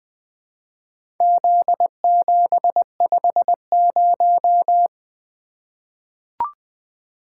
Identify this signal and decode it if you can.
Morse code, heard as a keyed tone, about 20 wpm, 700 Hz. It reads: Z750
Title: Four Letter-Number Combinations - 20wpm